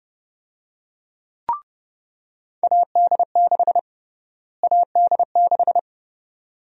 Morse code, heard as a keyed tone, about 30 wpm, 700 Hz. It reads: AD6 AD6
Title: USA Call Sign Prefixes - Repeated 2x - 30wpm